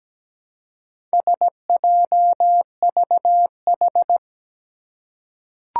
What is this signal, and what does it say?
Morse code, heard as a keyed tone, about 17 wpm, 700 Hz. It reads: SJVH
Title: Four Challenging Letter-Number Combinations 17wpm